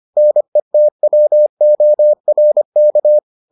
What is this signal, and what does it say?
Morse code, heard as a keyed tone, about 25 wpm, 600 Hz. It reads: NETWORK